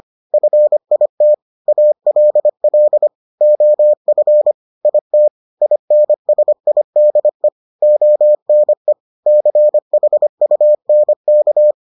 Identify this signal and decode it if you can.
Morse code, heard as a keyed tone, about 25 wpm, 600 Hz. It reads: FIT ALL OF IT INSIDE ONE CHUNK